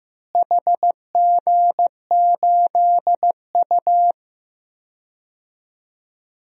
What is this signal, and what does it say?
Morse code, heard as a keyed tone, about 16 wpm, 700 Hz. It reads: HG8U
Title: DXCC-100 Call Signs - 4 Characters - Repeated 3x - 15wpm